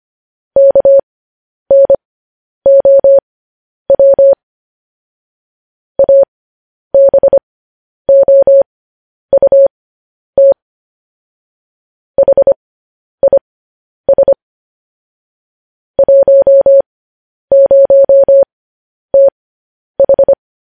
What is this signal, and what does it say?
Morse code, heard as a keyed tone, about 25 wpm, spaced as for 10 wpm, 550 Hz. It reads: KNOW ABOUT HIS 10TH
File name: podcast / News headlines-1x-25x10